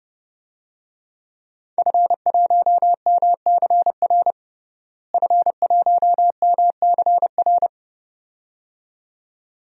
Morse code, heard as a keyed tone, about 30 wpm, 700 Hz. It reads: F1MCR F1MCR
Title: DXCC-100 Call Signs - 5 Characters - Repeated 2x - 30wpm